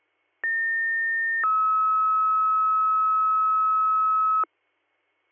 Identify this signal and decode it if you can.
Two-tone page, heard as a one-second tone, then a three-second tone, about 1800 then 1290 Hz. It is heard from 0.4 to 4.4 s.